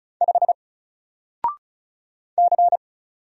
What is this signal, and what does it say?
Morse code, heard as a keyed tone, about 36 wpm, 700 Hz. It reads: 5 C
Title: Single Character (Letter:Number:Punctuation:Prosign) 35wpm